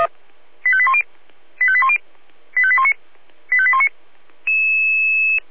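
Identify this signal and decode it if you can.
Five-tone selective call (ZVEI), heard as quick runs of five tones, tones of about 70 ms. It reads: 85810 85810 85819 85819
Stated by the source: ZVEI five-tone selective call received by SDR.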